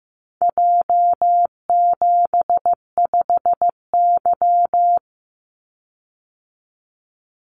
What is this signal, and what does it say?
Morse code, heard as a keyed tone, about 16 wpm, 700 Hz. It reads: J75Y
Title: Call Signs Repeated 3x - 15wpm